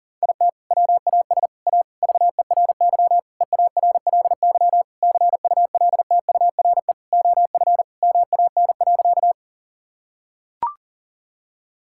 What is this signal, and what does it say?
Morse code, heard as a keyed tone, about 40 wpm, 690 Hz. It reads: IT WAS A VERY EARLY CULTURE OF MAN.
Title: Sentences from Top 1000 Words - Review Part 6 - 40wpm